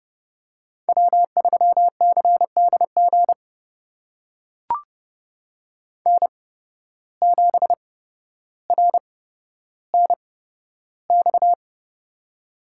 Morse code, heard as a keyed tone, about 30 wpm, spaced as for 13 wpm, 700 Hz. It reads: W3CDG N7RNX
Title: USA Call Signs - 8x Character Spacing - 30wpm